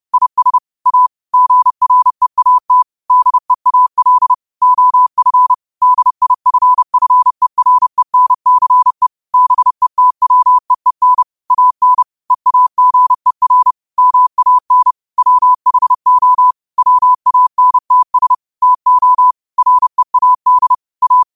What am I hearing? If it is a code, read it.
Morse code, heard as a keyed tone, about 30 wpm, 1000 Hz. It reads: IS A GREAT DEAL OF DIFFERENCE BETWEEN AN EAGER MAN WHO WANTS TO READ A
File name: qotdfiles / 30wpm